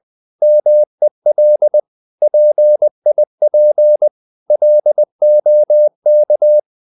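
Morse code, heard as a keyed tone, about 20 wpm, 600 Hz. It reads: MEL PIP LOK
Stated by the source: Morse code practice of short Danish words at 20 wpm, from CW med Gnister.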